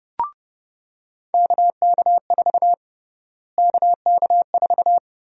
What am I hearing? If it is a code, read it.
Morse code, heard as a keyed tone, about 30 wpm, 700 Hz. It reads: KK4 KK4
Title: USA Call Sign Prefixes - Repeated 2x - 30wpm